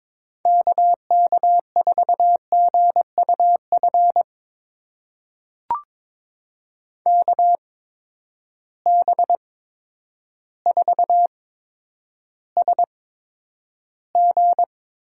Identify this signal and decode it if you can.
Morse code, heard as a keyed tone, about 22 wpm, spaced as for 10 wpm, 700 Hz. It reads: KK4GUF KB4SG
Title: USA Call Signs - 6 Characters - 8x Character Spacing - 22wpm